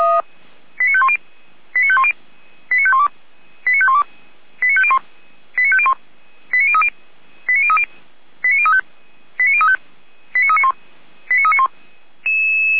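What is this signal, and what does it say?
Five-tone selective call (ZVEI), heard as quick runs of five tones, tones of about 70 ms. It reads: 89510 89510 89512 89512 89501 89501 89030 89035 89035 89391 89391
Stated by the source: ZVEI five-tone selective call received on SDR.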